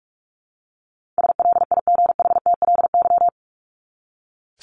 Morse code, heard as a keyed tone, about 44 wpm, 700 Hz. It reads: SLIGHTLY
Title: LY Suffix - Top 100 Words with Base Words 45wpm